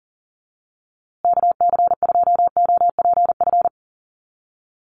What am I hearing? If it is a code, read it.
Morse code, heard as a keyed tone, about 40 wpm, 700 Hz. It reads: KC2OPF
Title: USA Call Signs Repeated 2x 40wpm